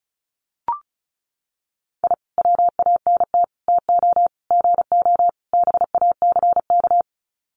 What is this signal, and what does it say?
Morse code, heard as a keyed tone, about 34 wpm, 700 Hz. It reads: I WANT TO GO BACK
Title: Sets of 5 Words Encore 35wpm